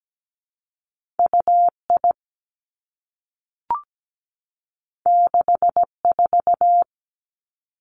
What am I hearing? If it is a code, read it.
Morse code, heard as a keyed tone, about 17 wpm, 700 Hz. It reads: UI 64
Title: Two Challenging Letter-Number Combinations 17wpm